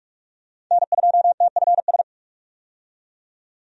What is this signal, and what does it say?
Morse code, heard as a keyed tone, about 47 wpm, 700 Hz. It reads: N2TFS
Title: USA Call Signs - 5 Characters - 45wpm